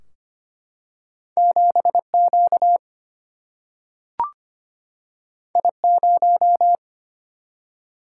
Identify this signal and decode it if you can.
Morse code, heard as a keyed tone, about 25 wpm, 700 Hz. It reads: 7Q I0
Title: Single-Letter Single-Number 25wpm